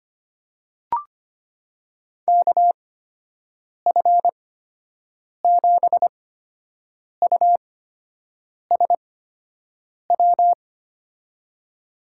Morse code, heard as a keyed tone, about 25 wpm, spaced as for 7 wpm, 700 Hz. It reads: KF7USW
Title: USA Call Signs - 8x Character Spacing - 25wpm